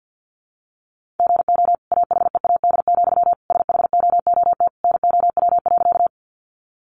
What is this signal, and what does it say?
Morse code, heard as a keyed tone, about 48 wpm, 700 Hz. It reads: GO AHEAD, SHOOT NOW.
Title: Sentences from Top 1200 Words 50wpm